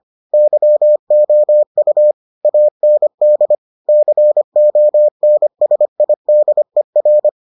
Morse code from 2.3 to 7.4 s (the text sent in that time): AND CONSIDER